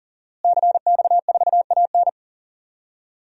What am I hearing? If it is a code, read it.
Morse code, heard as a keyed tone, about 39 wpm, 700 Hz. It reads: CX4AN
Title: DXCC-100 Call Signs - 5 Characters - Repeated 2x - 40wpm